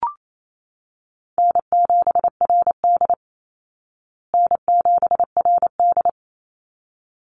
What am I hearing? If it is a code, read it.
Morse code, heard as a keyed tone, about 28 wpm, 700 Hz. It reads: N7RD N7RD
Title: Easy Call Signs Repeated 2x - 28wpm